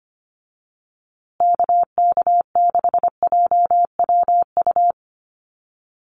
Morse code, heard as a keyed tone, about 25 wpm, 700 Hz. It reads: KK6JWU
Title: USA Call Signs Repeated 3x 25wpm